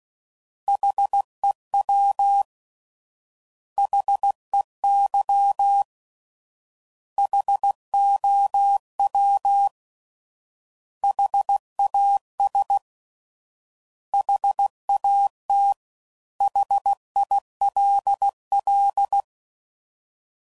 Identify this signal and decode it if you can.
Morse code, heard as a keyed tone, about 16 wpm, 800 Hz. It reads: HEW HEY HOW HAS HAT HILL